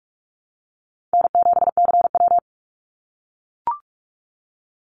Morse code, heard as a keyed tone, about 44 wpm, 700 Hz. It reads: N7CW
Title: NA SOTA Activator Callsigns 45wpm